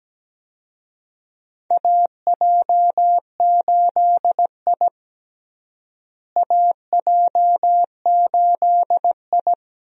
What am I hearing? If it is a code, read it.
Morse code, heard as a keyed tone, about 17 wpm, 700 Hz. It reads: AJ8I AJ8I
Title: USA Call Signs - 4 Characters - Repeated 2x - 17wpm